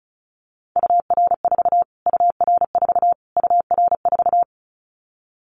Morse code, heard as a keyed tone, about 34 wpm, 700 Hz. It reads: UR4 UR4 UR4
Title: Callsign Prefixes for Top 25 DXCC - Repeated 3x - 35wpm